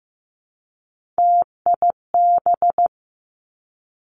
Morse code, heard as a keyed tone, about 15 wpm, 700 Hz. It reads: TIB
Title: Three Challenging Letter-Number Combinations 15wpm